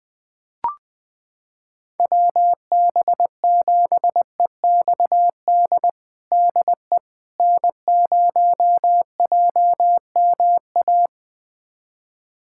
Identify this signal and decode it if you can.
Morse code, heard as a keyed tone, about 20 wpm, 700 Hz. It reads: WB7EXD DE N0JMA
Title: USA Callsign DE Callsign 20wpm